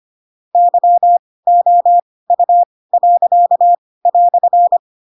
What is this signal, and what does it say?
Morse code, heard as a keyed tone, about 25 wpm, spaced as for 18 wpm, 700 Hz. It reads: YOU."
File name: Pinocchio / Pinocchio ch4 12 25 ik1hge